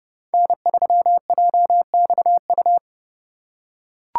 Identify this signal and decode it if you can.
Morse code, heard as a keyed tone, about 31 wpm, 700 Hz. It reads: N3JXU
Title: USA Call Signs - 5 Characters - 3x Character Spacing - 30wpm